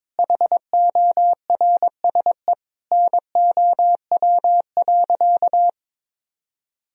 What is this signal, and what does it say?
Morse code, heard as a keyed tone, about 22 wpm, 700 Hz. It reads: HORSE NOW.
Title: Sentences from Top 1000 Words - Review Part 1 - 22wpm